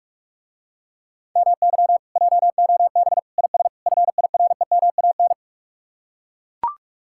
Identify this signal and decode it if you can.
Morse code, heard as a keyed tone, about 49 wpm, 700 Hz. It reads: MY JOB IS FIREMAN